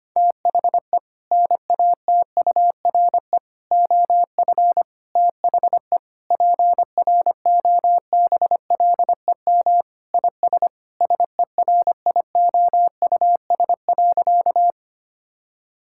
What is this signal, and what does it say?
Morse code, heard as a keyed tone, about 25 wpm, 700 Hz. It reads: THE NATURE OF THE PROBLEM IS SERIOUS.
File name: practice / 25wpm / Sentences with 3-Word Phrases - 25wpm